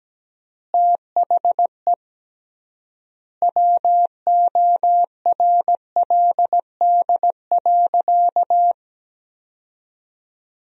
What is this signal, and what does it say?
Morse code, heard as a keyed tone, about 17 wpm, 700 Hz. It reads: THE WORLD.
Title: 3x Word-Spacing - Sentences from Top 200 Words - 17wpm